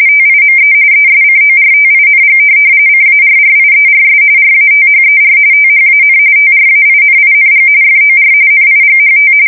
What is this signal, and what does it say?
RTTY, about 45.45 baud. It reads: TIME ESCORTING CARRIERS RATHER THAN TRAINING FOR NIGHT
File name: fsk170 / USS-New-Jersey